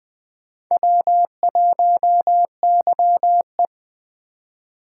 Morse code, heard as a keyed tone, about 20 wpm, 700 Hz. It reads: W1YE
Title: USA Call Signs - 4 Characters - 4x Character Spacing - 20wpm